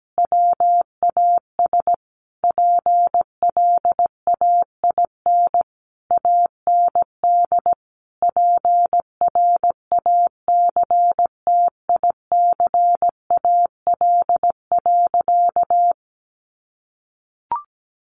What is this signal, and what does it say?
Morse code, heard as a keyed tone, about 17 wpm, 700 Hz. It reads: WAS PLAIN AND PRACTICAL.
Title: Sentences from Top 1400 Words 17wpm